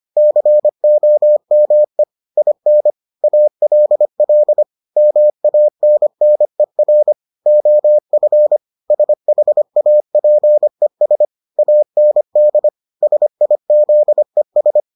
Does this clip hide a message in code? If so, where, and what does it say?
Yes: Morse code from 0.2 to 14.8 s, heard as a keyed tone, about 25 wpm, 600 Hz. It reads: COME IN ALL MANNER OF SHAPES AND SIZES